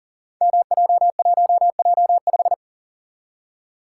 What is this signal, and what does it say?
Morse code, heard as a keyed tone, about 39 wpm, 700 Hz. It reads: MJ1J5
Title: Code Groups - 5 Letter:Numbers - 40wpm